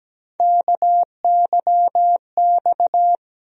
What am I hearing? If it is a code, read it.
Morse code, heard as a keyed tone, about 17 wpm, 700 Hz. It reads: KYX
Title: USA Call Sign Postfixes - 3 Letters 17wpm